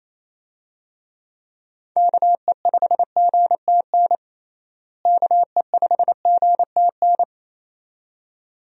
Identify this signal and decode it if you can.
Morse code, heard as a keyed tone, about 28 wpm, 700 Hz. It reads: KE5GTN KE5GTN
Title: USA Call Signs - 6 Characters - Repeated 2x - 28wpm